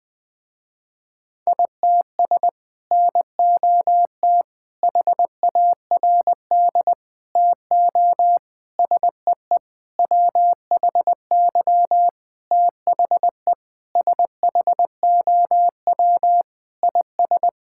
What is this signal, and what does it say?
Morse code, heard as a keyed tone, about 20 wpm, 700 Hz. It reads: ITS NOT HARD TO SEE WHY THE SHOW IS
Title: Sentences with 6-Word Phrases - Spoken Beginning - 20wpm